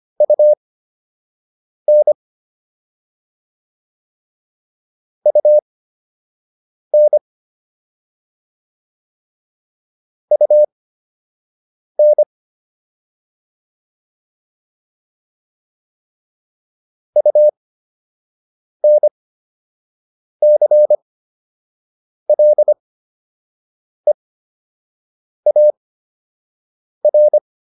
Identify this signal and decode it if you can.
Morse code, heard as a keyed tone, about 25 wpm, spaced as for 6 wpm, 600 Hz. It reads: UN UN UN UNCLEAR
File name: UN7 06